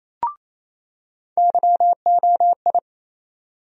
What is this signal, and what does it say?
Morse code, heard as a keyed tone, about 28 wpm, 700 Hz. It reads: YOI
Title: USA Call Sign Postfixes - 3 Letters 28wpm